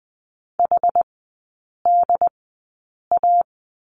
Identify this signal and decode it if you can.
Morse code, heard as a keyed tone, about 20 wpm, spaced as for 8 wpm, 700 Hz. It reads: HDA
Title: 107 Lesson 18e - Letter m - Three Characters 20wpm